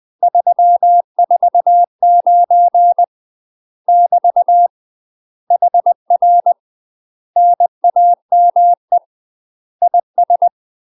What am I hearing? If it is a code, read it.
Morse code, heard as a keyed tone, about 20 wpm, 700 Hz. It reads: 349 <BT> HR NAME IS
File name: UK MASTER QSOs TRAINING / 20WPM / UK MASTER QSOs 0014